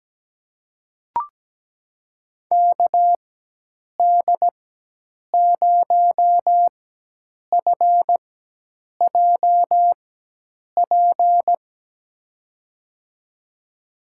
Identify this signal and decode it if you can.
Morse code, heard as a keyed tone, about 17 wpm, spaced as for 8 wpm, 700 Hz. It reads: KD0FJP
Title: USA Call Signs - 6 Characters - 4x Character Spacing - 17wpm